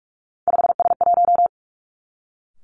Morse code, heard as a keyed tone, about 46 wpm, 700 Hz. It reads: 5S1
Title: Three Challenging Letter-Number Combinations 45wpm